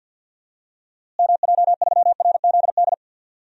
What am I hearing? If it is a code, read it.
Morse code, heard as a keyed tone, about 51 wpm, 700 Hz. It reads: MJ3RZD
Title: DXCC-100 Call Signs - 6 Characters - Repeated 3x - 50wpm